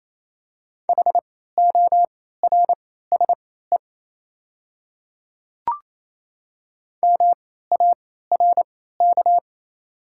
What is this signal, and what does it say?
Morse code, heard as a keyed tone, about 28 wpm, spaced as for 16 wpm, 700 Hz. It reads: HORSE MARK
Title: 3x Character Spacing - Top 300 Words - 28wpm